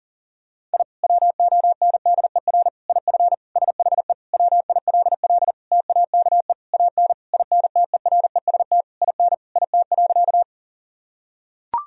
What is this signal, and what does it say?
Morse code, heard as a keyed tone, about 37 wpm, 700 Hz. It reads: I WONDER IF SHE WILL TAKE AN INTEREST IN IT.
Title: Sentences from Top 1000 Words - Review Part 3 - 40wpm